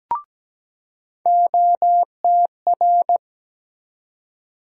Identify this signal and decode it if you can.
Morse code, heard as a keyed tone, about 17 wpm, 700 Hz. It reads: OTR
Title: USA Call Sign Postfixes - 3 Letters 17wpm